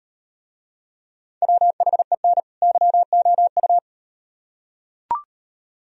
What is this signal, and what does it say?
Morse code, heard as a keyed tone, about 38 wpm, 700 Hz. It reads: WHEN YOU